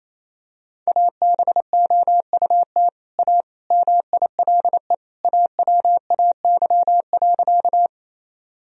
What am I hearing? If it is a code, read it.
Morse code, heard as a keyed tone, about 28 wpm, 700 Hz. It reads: ABOUT A MILE AWAY.
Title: Sentences from Top 1000 Words - Review Part 1 - 28wpm